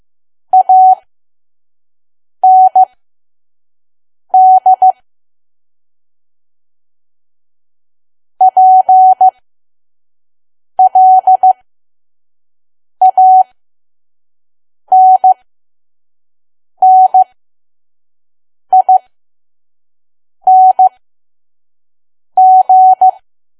Morse code, heard as a keyed tone, about 15 wpm, spaced as for 5 wpm, 750 Hz. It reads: AND PLANNING
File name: Archive / 5 WPM files / PM